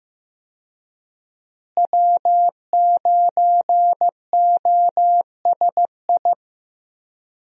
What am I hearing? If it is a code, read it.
Morse code, heard as a keyed tone, about 15 wpm, 700 Hz. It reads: W9OSI
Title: USA Call Signs - 5 Characters - Repeated 3x - 15wpm